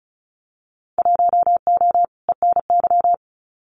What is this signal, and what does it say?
Morse code, heard as a keyed tone, about 36 wpm, 700 Hz. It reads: 1O ENY
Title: Winter Field Day Exchanges 35wpm